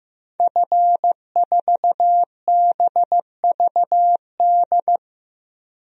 Morse code, heard as a keyed tone, about 15 wpm, 700 Hz. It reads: F4BVD
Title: DXCC-100 Call Signs - 5 Characters - 15wpm